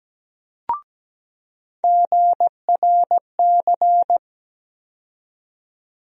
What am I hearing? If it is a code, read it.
Morse code, heard as a keyed tone, about 17 wpm, 700 Hz. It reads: GRC
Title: USA Call Sign Postfixes - 3 Letters 17wpm